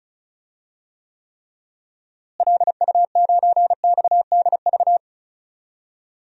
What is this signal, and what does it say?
Morse code, heard as a keyed tone, about 34 wpm, 700 Hz. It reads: LU9XDV
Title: DXCC-100 Call Signs - 6 Characters - Repeated 3x - 35wpm